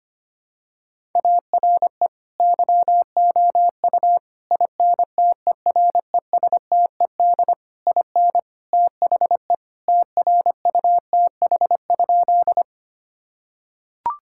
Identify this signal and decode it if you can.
Morse code, heard as a keyed tone, about 25 wpm, 700 Hz. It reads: ARE YOU INTERESTED IN THE TRUTH?
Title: Sentences from Top 1000 Words - Review Part 6 - 25wpm